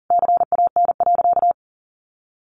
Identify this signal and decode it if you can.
Morse code, heard as a keyed tone, about 40 wpm, 700 Hz. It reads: CAN.
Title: Sentences from Top 1400 Words 40wpm